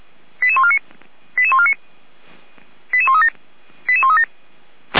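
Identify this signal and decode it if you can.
Five-tone selective call (ZVEI), heard as quick runs of five tones, tones of about 70 ms. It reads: 88149 88149 88137 88137
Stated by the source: ZVEI five-tone call received by SDR.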